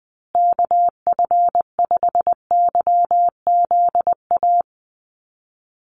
Morse code, heard as a keyed tone, about 20 wpm, 700 Hz. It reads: KF5YZA
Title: USA Call Signs Repeated 3x 20wpm